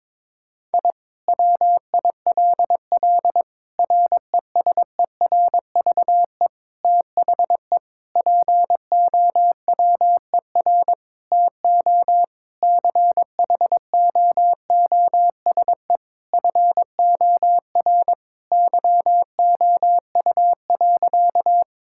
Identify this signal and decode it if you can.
Morse code, heard as a keyed tone, about 22 wpm, 700 Hz. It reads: I WILL RESERVE THE POWER TO CHOOSE FOR YOU.
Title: Sentences from Top 1000 Words - Review Part 6 - 22wpm